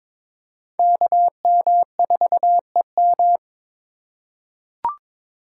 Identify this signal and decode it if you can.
Morse code, heard as a keyed tone, about 22 wpm, 700 Hz. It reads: KM4EM
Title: USA Call Signs - 5 Characters - 4x Character Spacing - 22wpm